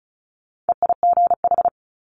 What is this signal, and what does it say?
Morse code, heard as a keyed tone, about 34 wpm, 700 Hz. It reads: EIGH
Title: Multi-letter Phonemes Quick Fire 35wpm